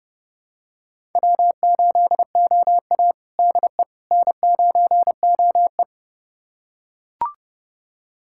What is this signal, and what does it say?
Morse code, heard as a keyed tone, about 32 wpm, 700 Hz. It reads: W8OA DE N9OE
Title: USA Callsign DE Callsign - 4 Characters - 30wpm